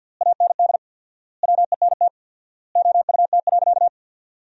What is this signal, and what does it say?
Morse code, heard as a keyed tone, about 50 wpm, 700 Hz. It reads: AND WENT OUT.
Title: 4x Word-Spacing - Sentences from Top 300 Words - 50wpm